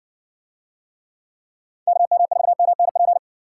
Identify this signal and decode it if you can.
Morse code, heard as a keyed tone, about 58 wpm, 700 Hz. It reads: KM4MNP